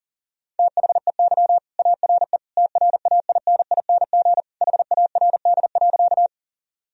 Morse code, heard as a keyed tone, about 40 wpm, 700 Hz. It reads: THEY ARE TRAINING HARD.
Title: Sentences from Top 1000 Words - Review Part 4 - 40wpm